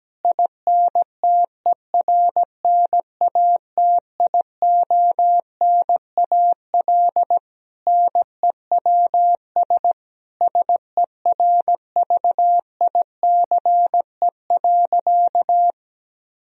Morse code, heard as a keyed tone, about 17 wpm, 700 Hz. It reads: INTERNATIONAL NEWS SERVICE.
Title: Sentences from Top 1000 Words - Review Part 4 - 17wpm